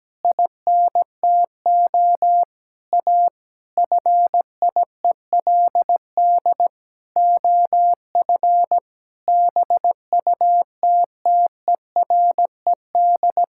Morse code, heard as a keyed tone, about 17 wpm, 700 Hz. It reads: INTO A FIELD OF BUTTERED
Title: Pangram Challenge v2 - 17wpm